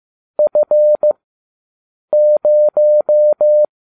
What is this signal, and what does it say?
Morse code, heard as a keyed tone, about 15 wpm, 600 Hz. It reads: F 0